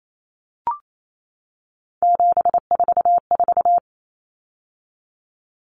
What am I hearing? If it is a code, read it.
Morse code, heard as a keyed tone, about 28 wpm, 700 Hz. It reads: 744
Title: Numbers - 3-Digit 28wpm